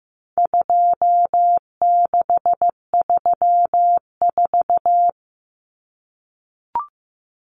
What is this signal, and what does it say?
Morse code, heard as a keyed tone, about 15 wpm, 700 Hz. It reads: 2634
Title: Numbers - 4-Digit 15wpm